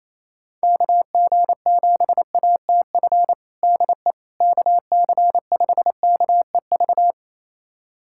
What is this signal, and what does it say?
Morse code, heard as a keyed tone, about 28 wpm, 700 Hz. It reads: KG7ATF DE KC5KEV
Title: USA Callsign DE Callsign - 6 Characters - 28wpm